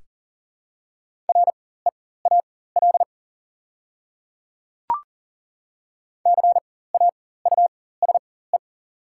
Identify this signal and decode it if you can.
Morse code, heard as a keyed tone, about 38 wpm, spaced as for 18 wpm, 700 Hz. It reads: REAL CAUSE
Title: 4x Character Spacing - Top 300 Words - 40wpm